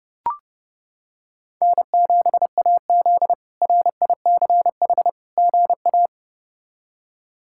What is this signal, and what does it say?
Morse code, heard as a keyed tone, about 30 wpm, 700 Hz. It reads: N7AZ RICH GA
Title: K1USN SST QSO Practice - 30wpm